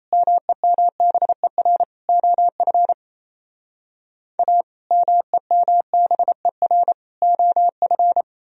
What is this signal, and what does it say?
Morse code, heard as a keyed tone, about 35 wpm, 700 Hz. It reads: MEMBER OF A MEMBER OF
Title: Speed-Racing - Sets of 3 Words 28wpm